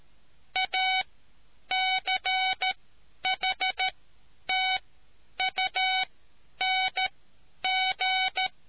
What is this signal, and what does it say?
Morse code, heard as a keyed tone, about 13 wpm, spaced as for 9 wpm, 750 Hz. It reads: ACHTUNG